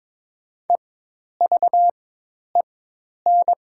Morse code, heard as a keyed tone, about 22 wpm, spaced as for 10 wpm, 700 Hz. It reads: EVEN
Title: 4x Character Spacing - Top 200 Words - 22wpm